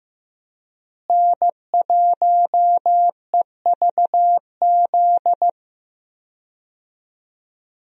Morse code, heard as a keyed tone, about 16 wpm, 700 Hz. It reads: N1EVZ
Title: USA Call Signs - 5 Characters - Repeated 3x - 15wpm